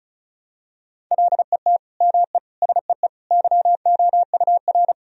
Morse code, heard as a keyed tone, about 36 wpm, 700 Hz. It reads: LET ME SEE YOUR
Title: Sentences from Top 1000 Words - Review Part 2 - 35wpm